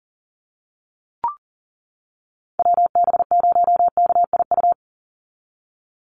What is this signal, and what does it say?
Morse code, heard as a keyed tone, about 40 wpm, 700 Hz. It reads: WB0KIU
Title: NA SOTA Chaser Callsigns 40wpm